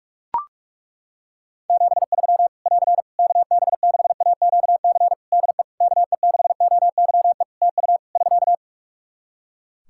Morse code, heard as a keyed tone, about 41 wpm, 700 Hz. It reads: 73 <AR> KB6AQC DE KE6OYE TU <SK>
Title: Practice QSO - Exchange 4 - Ending - 45wpm